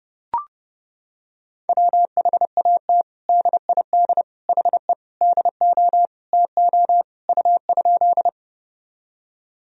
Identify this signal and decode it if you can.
Morse code, heard as a keyed tone, about 31 wpm, 700 Hz. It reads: WHAT DID HE DO TO U?
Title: Sentences from Top 100 Words and Abbreviations 30wpm